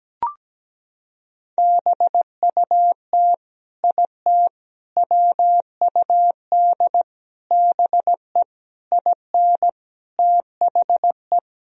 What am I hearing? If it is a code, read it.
Morse code, heard as a keyed tone, about 17 wpm, 700 Hz. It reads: BUT IT WUD BE IN THE
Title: Sentences from Top 300 Words and Abbreviations 17wpm